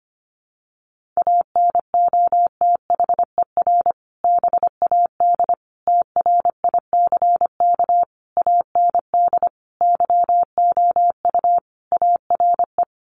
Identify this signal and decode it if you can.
Morse code, heard as a keyed tone, about 25 wpm, 700 Hz. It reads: ANOTHER BAD TRICK AND YOU ARE
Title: Sentences from Top 1500 Words 25wpm